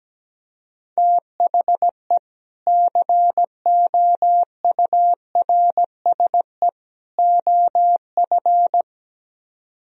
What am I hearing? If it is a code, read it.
Morse code, heard as a keyed tone, about 17 wpm, 700 Hz. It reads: THE COURSE OF
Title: Speed-Racing - Sets of 3 Words 17wpm